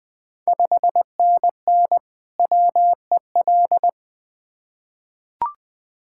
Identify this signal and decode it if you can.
Morse code, heard as a keyed tone, about 20 wpm, 700 Hz. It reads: 5NN WEL
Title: Ontario QSO Party Exchange 20wpm